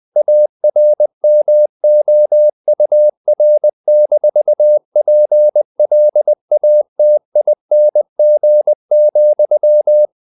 Morse code, heard as a keyed tone, about 20 wpm, 600 Hz. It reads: ARMOUR-PLATING,
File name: the old man and the sea / the old man and the sea part 1 20wpm